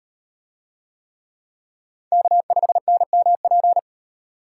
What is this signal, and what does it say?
Morse code, heard as a keyed tone, about 38 wpm, 700 Hz. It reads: K5NMP